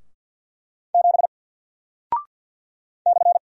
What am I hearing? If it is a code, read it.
Morse code, heard as a keyed tone, about 49 wpm, 700 Hz. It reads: 7 /
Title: Single Character (Letter:Number:Punctuation:Prosign) 50wpm